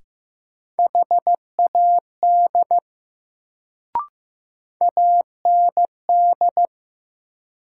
Morse code, heard as a keyed tone, about 15 wpm, 700 Hz. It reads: HAD AND
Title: Sentences from Top 100 Words - New Word Drill - 15wpm